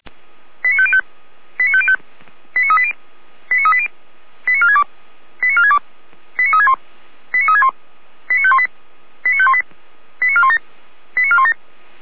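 Five-tone selective call (ZVEI), heard as quick runs of five tones, tones of about 70 ms. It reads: 89505 89505 89380 89380 89462 89462 89361 89361 89518 89518 89417 89417
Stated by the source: ZVEI five-tone selective call received via SDR.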